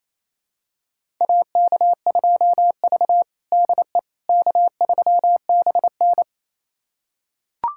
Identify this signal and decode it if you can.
Morse code, heard as a keyed tone, about 28 wpm, 700 Hz. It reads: AK2V DE K3BN
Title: USA Callsign DE Callsign - 4 Characters - 28wpm